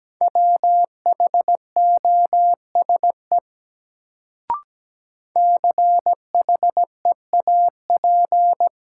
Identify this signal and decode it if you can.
Morse code, heard as a keyed tone, about 17 wpm, 700 Hz. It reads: WHOSE CHEAP
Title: Sentences from Top 1200 Words - New Word Drill - 17wpm